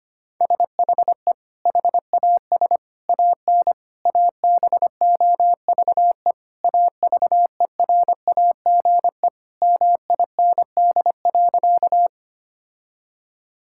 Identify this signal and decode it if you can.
Morse code, heard as a keyed tone, about 25 wpm, 700 Hz. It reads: SHE HAS AN ABOVE AVERAGE MIND.
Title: Sentences from Top 1000 Words - Review Part 6 - 25wpm